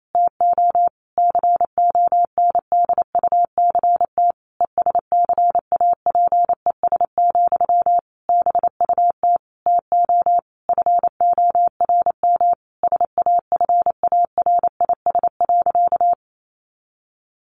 Morse code, heard as a keyed tone, about 28 wpm, 700 Hz. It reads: TO CONDUCT ESCAPES, BUT TO FORM SAFARIS.
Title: A Sound Of Thunder 28wpm